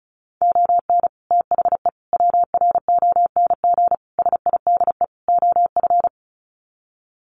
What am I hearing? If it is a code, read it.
Morse code, heard as a keyed tone, about 34 wpm, 700 Hz. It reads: ON THE WRONG SIDE OF
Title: Sets of 5 Words Encore 35wpm